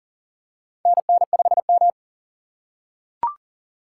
Morse code, heard as a keyed tone, about 40 wpm, 700 Hz. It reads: NN5M
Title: USA Call Signs - 6x Character Spacing - 40wpm